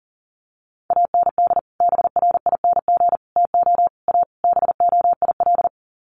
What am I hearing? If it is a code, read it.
Morse code, heard as a keyed tone, about 40 wpm, 700 Hz. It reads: AND BRING TO A BOIL
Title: Sets of 5 Words Encore 40wpm